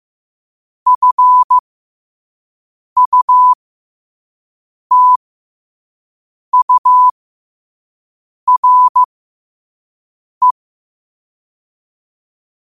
Morse code, heard as a keyed tone, about 15 wpm, spaced as for 5 wpm, 1000 Hz. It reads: FUTURE
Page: New quotes every day in morse code at 5 Words per minute.